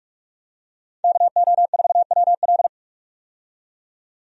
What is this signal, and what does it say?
Morse code, heard as a keyed tone, about 46 wpm, 700 Hz. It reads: KO4WL
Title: USA Call Signs - 5 Characters - 45wpm